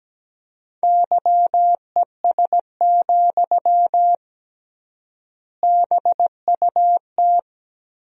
Morse code, heard as a keyed tone, about 17 wpm, 700 Hz. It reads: YES, BUT
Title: 3x Word-Spacing - Sentences from Top 500 Words - 17wpm